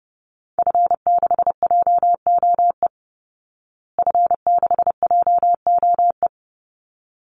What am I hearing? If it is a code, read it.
Morse code, heard as a keyed tone, about 30 wpm, 700 Hz. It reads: F6JOE F6JOE
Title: Call Signs Repeated 3x - 30wpm